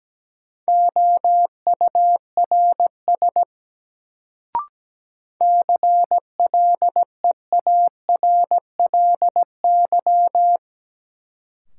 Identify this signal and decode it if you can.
Morse code, heard as a keyed tone, about 17 wpm, 700 Hz. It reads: OURS CLEARLY
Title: Sentences from Top 1300 Words - New Word Drill Alt - 17wpm